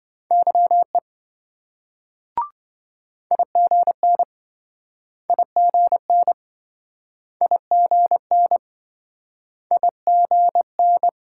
Morse code, heard as a keyed tone, about 27 wpm, 700 Hz. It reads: YE IGN IGN IGN IGN